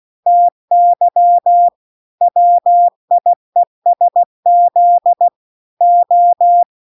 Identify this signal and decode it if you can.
Morse code, heard as a keyed tone, about 16 wpm, 700 Hz. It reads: TY WIESZ O